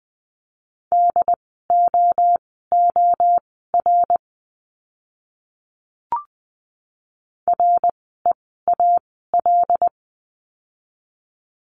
Morse code, heard as a keyed tone, about 20 wpm, spaced as for 15 wpm, 700 Hz. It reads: DOOR REAL
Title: 101 Lesson 17f - Letter u - Words 20wpm